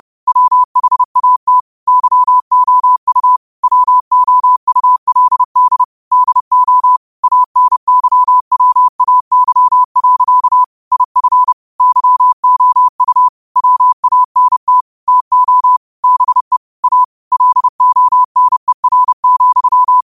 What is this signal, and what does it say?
Morse code, heard as a keyed tone, about 30 wpm, 1000 Hz. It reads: WHAT YOU WOULD DO ANYWAY. IF YOU WANT TO BE A LONER,
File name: qotdfiles / 30wpm